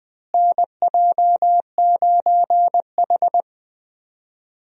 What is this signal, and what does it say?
Morse code, heard as a keyed tone, about 20 wpm, 700 Hz. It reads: NJ9H
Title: USA Call Signs - 4 Characters - Repeated 3x - 20wpm